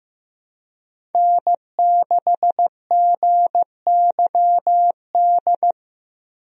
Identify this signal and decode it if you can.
Morse code, heard as a keyed tone, about 15 wpm, 700 Hz. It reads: N6GYD
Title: USA Call Signs - 5 Characters - Repeated 2x - 15wpm